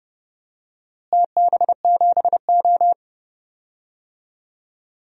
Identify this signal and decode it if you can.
Morse code, heard as a keyed tone, about 32 wpm, 700 Hz. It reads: TB7O
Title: DXCC-100 Call Signs - 4 Characters - Repeated 2x - 30wpm